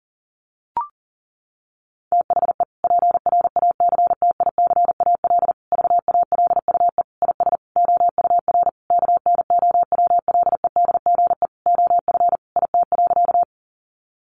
Morse code, heard as a keyed tone, about 40 wpm, 700 Hz. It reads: THE PRACTICAL VALUE IS OUR KNOWLEDGE OF IT.
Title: Sentences from Top 1400 Words 40wpm